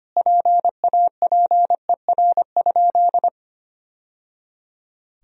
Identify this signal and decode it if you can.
Morse code, heard as a keyed tone, about 25 wpm, 700 Hz. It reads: PAPER?
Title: Sentences from Top 1000 Words - Review Part 2 - 25wpm